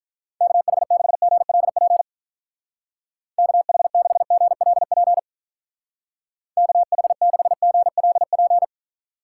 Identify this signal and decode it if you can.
Morse code, heard as a keyed tone, about 45 wpm, 700 Hz. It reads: KH6GLP KH6GLP KH6GLP